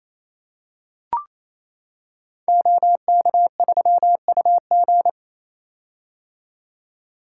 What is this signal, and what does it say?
Morse code, heard as a keyed tone, about 28 wpm, 700 Hz. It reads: OK3UG
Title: Code Groups - 5 Letter:Numbers - 28wpm